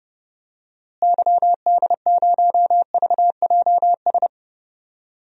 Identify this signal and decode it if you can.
Morse code, heard as a keyed tone, about 30 wpm, 700 Hz. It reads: YD0VJS
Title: DXCC-100 Call Signs - 6 Characters - Repeated 2x - 30wpm